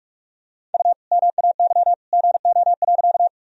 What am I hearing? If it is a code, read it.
Morse code, heard as a keyed tone, about 46 wpm, 700 Hz. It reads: U MAY GO.
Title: Sentences from Top 100 Words and Abbreviations 45wpm